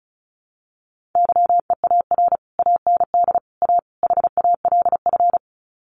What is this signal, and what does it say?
Morse code, heard as a keyed tone, about 34 wpm, 700 Hz. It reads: YEAR AND A HALF
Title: Sets of 4 Words 35wpm